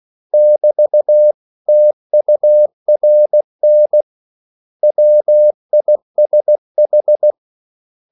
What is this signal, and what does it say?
Morse code, heard as a keyed tone, about 16 wpm, 600 Hz. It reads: <BT> TURN WISH
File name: CW / Practice / 500MostCommonEnglishWords16